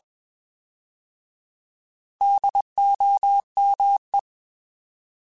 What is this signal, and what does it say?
Morse code, heard as a keyed tone, about 21 wpm, 800 Hz. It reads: DOME
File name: Alphabetical / Alphabetical mp3s / Ds @20